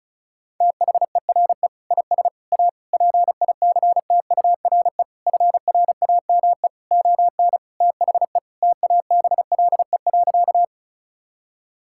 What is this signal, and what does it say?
Morse code, heard as a keyed tone, about 36 wpm, 700 Hz. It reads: THERE IS A PICTURE FRAME ON THE TABLE.
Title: Sentences from Top 1000 Words - Review Part 6 - 35wpm